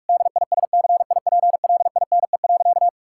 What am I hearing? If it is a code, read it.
Morse code, heard as a keyed tone, about 41 wpm, 700 Hz. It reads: DISCIPLINE.
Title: Sentences from Top 1000 Words - Review Part 6 - 45wpm